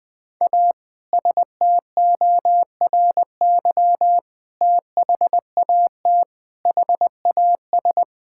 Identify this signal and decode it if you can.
Morse code, heard as a keyed tone, about 20 wpm, 700 Hz. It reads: A STORY THAT HAS
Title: Sentences with 2-Word Phrases - 20wpm